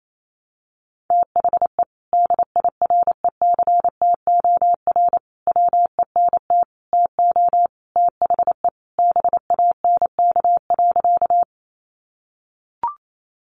Morse code, heard as a keyed tone, about 28 wpm, 700 Hz. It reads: THE DIRECTOR WENT TO THE BANK.
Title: Sentences from Top 1300 Words 28wpm